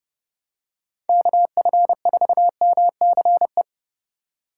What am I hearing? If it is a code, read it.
Morse code, heard as a keyed tone, about 30 wpm, 700 Hz. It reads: KF4MCE